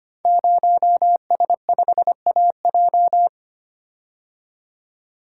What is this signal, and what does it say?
Morse code, heard as a keyed tone, about 25 wpm, 700 Hz. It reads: 0S5AJ
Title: Code Groups - 5 Letter:Numbers - 25wpm